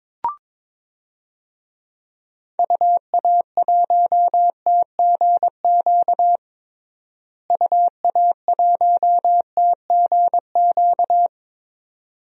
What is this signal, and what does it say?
Morse code, heard as a keyed tone, about 22 wpm, 700 Hz. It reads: UA1TGQ UA1TGQ
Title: DXCC-100 Call Signs - 6 Characters - Repeated 2x - 22wpm